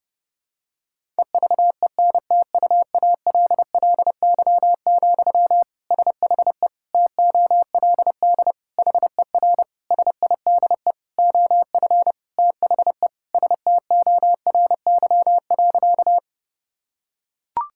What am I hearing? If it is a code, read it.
Morse code, heard as a keyed tone, about 30 wpm, 700 Hz. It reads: EVENTUALLY, SHE TOLD HER SIDE OF THE STORY.
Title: Sentences from Top 1000 Words - Review Part 6 - 30wpm